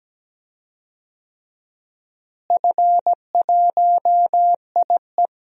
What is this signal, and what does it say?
Morse code, heard as a keyed tone, about 17 wpm, 700 Hz. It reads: F1IE
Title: DXCC-100 Call Signs - 4 Characters - Repeated 3x - 17wpm